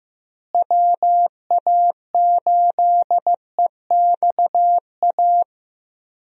Morse code, heard as a keyed tone, about 15 wpm, 700 Hz. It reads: WA8EXA
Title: USA Call Signs - 6 Characters - Repeated 2x - 15wpm